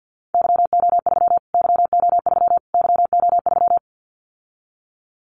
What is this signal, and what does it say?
Morse code, heard as a keyed tone, about 50 wpm, 700 Hz. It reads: YO3 YO3 YO3
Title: Callsign Prefixes for Top 25 DXCC - Repeated 3x - 50wpm